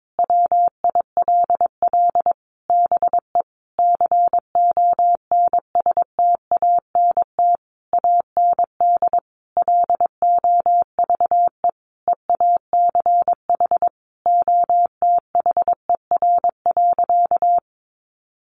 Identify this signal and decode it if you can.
Morse code, heard as a keyed tone, about 22 wpm, 700 Hz. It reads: WILL BE CONSTANT AND LOVE EACH OTHER.
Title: Sentences from Top 1500 Words 22wpm